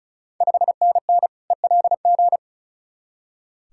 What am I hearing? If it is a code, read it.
Morse code, heard as a keyed tone, about 36 wpm, 700 Hz. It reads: 5NN ELG
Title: Ontario QSO Party Exchange 35wpm